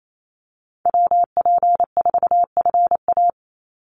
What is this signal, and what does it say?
Morse code, heard as a keyed tone, about 28 wpm, 700 Hz. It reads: WP4FA
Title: USA Call Signs 28wpm